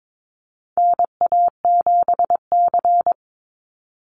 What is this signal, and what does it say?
Morse code, heard as a keyed tone, about 22 wpm, 700 Hz. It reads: NA7C
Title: Call Signs - 22wpm